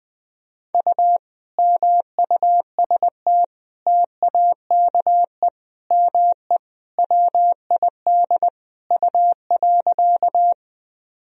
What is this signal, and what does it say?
Morse code, heard as a keyed tone, about 20 wpm, 700 Hz. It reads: U MUST TAKE ME WID U.
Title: Sentences from Top 200 Words and Abbreviations 20wpm